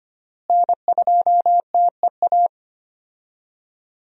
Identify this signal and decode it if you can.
Morse code, heard as a keyed tone, about 25 wpm, 700 Hz. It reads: N2TEA
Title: USA Call Signs - 5 Characters - Repeated 2x - 25wpm